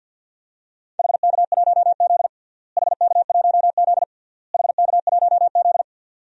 Morse code, heard as a keyed tone, about 53 wpm, 700 Hz. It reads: HK1Z HK1Z HK1Z
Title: DXCC-100 Call Signs - 4 Characters - Repeated 3x - 50wpm